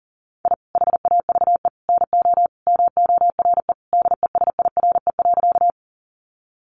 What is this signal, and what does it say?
Morse code, heard as a keyed tone, about 40 wpm, 700 Hz. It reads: I HAVE NO MORE DESIRE.
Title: Sentences from Top 1100 Words 40wpm